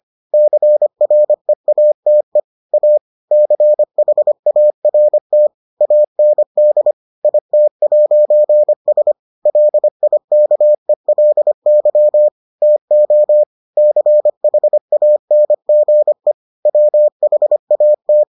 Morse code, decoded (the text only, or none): CREATE A CHART AND IT'S LIKELY TO CHANGE WHAT